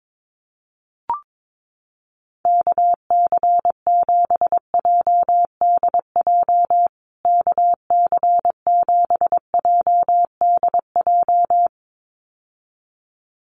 Morse code, heard as a keyed tone, about 22 wpm, 700 Hz. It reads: KC7JDJ KC7JDJ
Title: USA Call Signs Repeated 2x 22wpm